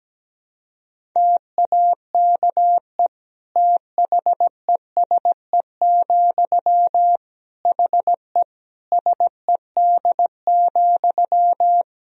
Morse code, heard as a keyed tone, about 17 wpm, 700 Hz. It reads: TAKE THESE, HE SED,
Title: Sentences from Top 300 Words and Abbreviations 17wpm